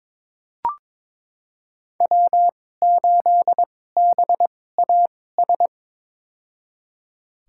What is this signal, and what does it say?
Morse code, heard as a keyed tone, about 21 wpm, spaced as for 16 wpm, 700 Hz. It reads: W8BAS
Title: USA Call Signs - 2x Character Spacing - 22wpm